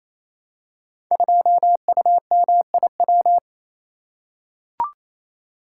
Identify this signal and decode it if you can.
Morse code, heard as a keyed tone, about 29 wpm, 700 Hz. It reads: 2UMIW
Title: Code Groups - 5 Letter:Numbers - 28wpm